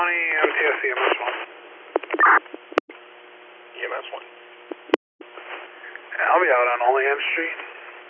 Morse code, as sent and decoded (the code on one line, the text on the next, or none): none